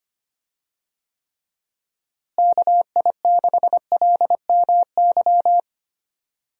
Morse code, heard as a keyed tone, about 25 wpm, 700 Hz. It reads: KI6LMY
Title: USA Call Signs - 6 Characters - Repeated 3x - 25wpm